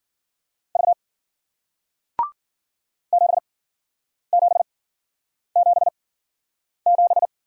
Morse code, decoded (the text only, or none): V 7777